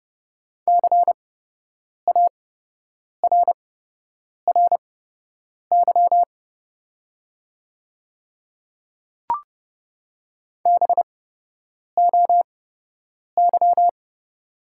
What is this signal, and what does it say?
Morse code, heard as a keyed tone, about 30 wpm, spaced as for 8 wpm, 690 Hz. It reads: CARRY BOY
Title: 8x Character Spacing - Top 300 Words - 30wpm